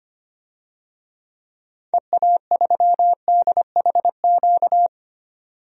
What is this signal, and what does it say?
Morse code, heard as a keyed tone, about 25 wpm, 700 Hz. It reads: EA3DHQ
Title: DXCC-100 Call Signs - 6 Characters - 25wpm